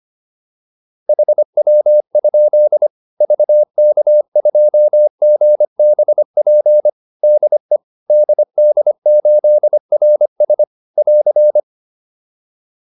Morse code, heard as a keyed tone, about 25 wpm, 600 Hz. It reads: HW? VK2GBP DE DD8RS <AR>
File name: QSO405a 25